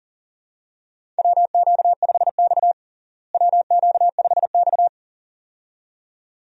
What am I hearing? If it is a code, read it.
Morse code, heard as a keyed tone, about 40 wpm, 700 Hz. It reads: WQ5X WQ5X
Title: USA Call Signs - 4 Characters - Repeated 2x - 40wpm